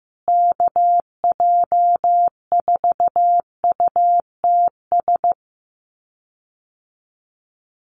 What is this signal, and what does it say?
Morse code, heard as a keyed tone, about 15 wpm, 700 Hz. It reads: KJ4UTS
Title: USA Call Signs Repeated 3x 15wpm